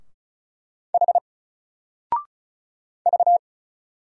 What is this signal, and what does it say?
Morse code, heard as a keyed tone, about 33 wpm, 700 Hz. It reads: H V
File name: Single-Letter-Number-35wpm